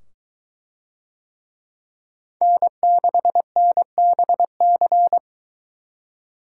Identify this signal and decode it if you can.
Morse code, heard as a keyed tone, about 22 wpm, 700 Hz. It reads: N6NBC